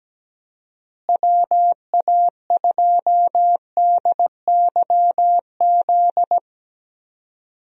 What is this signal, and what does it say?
Morse code, heard as a keyed tone, about 17 wpm, 700 Hz. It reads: WA2DYZ
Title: USA Call Signs - 6 Characters - Repeated 3x - 17wpm